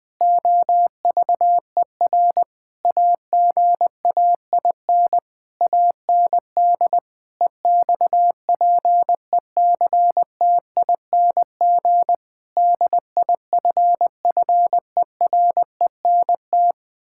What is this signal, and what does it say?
Morse code, heard as a keyed tone, about 20 wpm, 700 Hz. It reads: OVER AGAIN AND EXPECTING DIFFERENT
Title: Sentences with 9-Word Phrases - 20wpm